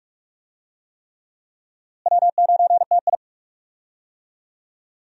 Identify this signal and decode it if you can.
Morse code, heard as a keyed tone, about 45 wpm, 700 Hz. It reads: W9TI
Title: USA Call Signs - 4 Characters - Repeated 3x - 45wpm